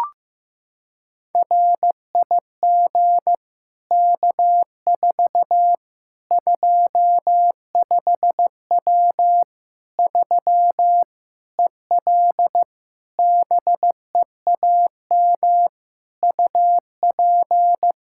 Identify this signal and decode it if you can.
Morse code, heard as a keyed tone, about 16 wpm, 700 Hz. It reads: RIG K4 25W 3 EL BEAM UP
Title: Practice QSO - Exchange 2 - Info - 15wpm